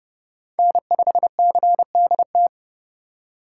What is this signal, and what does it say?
Morse code, heard as a keyed tone, about 29 wpm, 700 Hz. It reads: N5CDT